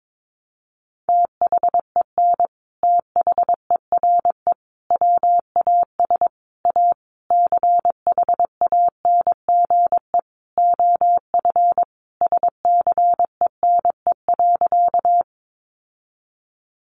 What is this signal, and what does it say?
Morse code, heard as a keyed tone, about 22 wpm, 700 Hz. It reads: THEN THERE WAS A CHANGE OF SCENE.
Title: Sentences from Top 1100 Words 22wpm